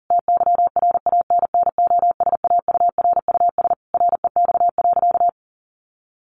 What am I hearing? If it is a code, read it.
Morse code, heard as a keyed tone, about 40 wpm, 700 Hz. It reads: TYRANNOSAURUS REX.
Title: A Sound Of Thunder 40wpm